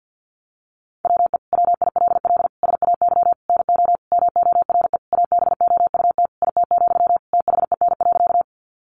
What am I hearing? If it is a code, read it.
Morse code, heard as a keyed tone, about 50 wpm, 700 Hz. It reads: WE WILL SAY NO MORE ABOUT IT, THEN.
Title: Sentences from Top 200 Words 50wpm